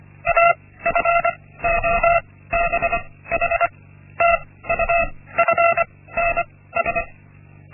Morse code, heard as a keyed tone, about 25 wpm, spaced as for 16 wpm, 680 Hz. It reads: AFOBH TUFNS
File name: Linksmakalnis-1